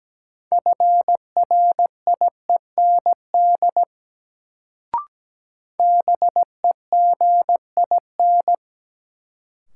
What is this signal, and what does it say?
Morse code, heard as a keyed tone, about 17 wpm, 700 Hz. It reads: FRIEND BEGIN
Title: Sentences from Top 300 Words - New Word Drill - 17wpm